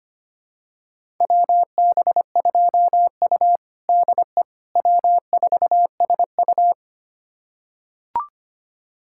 Morse code, heard as a keyed tone, about 25 wpm, 700 Hz. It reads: WB2U DE W4SU
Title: USA Callsign DE Callsign - 4 Characters - 25wpm